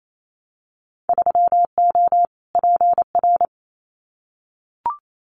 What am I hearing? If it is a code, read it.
Morse code, heard as a keyed tone, about 28 wpm, 700 Hz. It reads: 3O PR
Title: Winter Field Day Exchanges Encore 28wpm